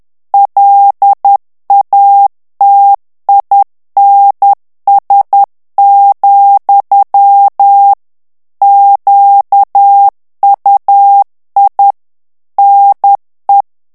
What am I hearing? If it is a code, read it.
Morse code, heard as a keyed tone, about 10 wpm, 800 Hz. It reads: LATINS, QUI NE
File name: FRANCAIS TEXTE CLAIR 2 VITESSE 10 MOTS MINUTES